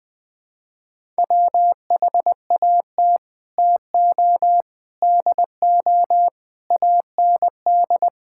Morse code, heard as a keyed tone, about 20 wpm, 700 Hz. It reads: WHAT TO DO AND
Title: Sets of 8 Words - 20wpm